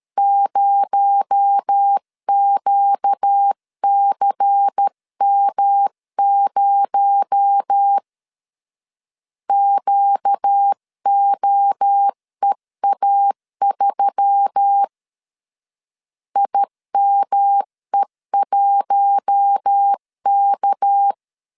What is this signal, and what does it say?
Morse code, heard as a keyed tone, about 13 wpm, 800 Hz. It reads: 0QCM0 QOEA3 IME1K
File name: CW / DET / 1 6 1500C S C 12wpm 3533